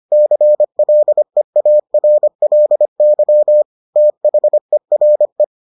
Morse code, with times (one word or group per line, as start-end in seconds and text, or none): 0.1-3.6 CLEARLY
4.0-5.4 THERE